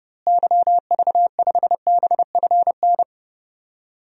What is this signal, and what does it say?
Morse code, heard as a keyed tone, about 31 wpm, 700 Hz. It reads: YV5BFN
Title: DXCC-100 Call Signs - 6 Characters - Repeated 2x - 30wpm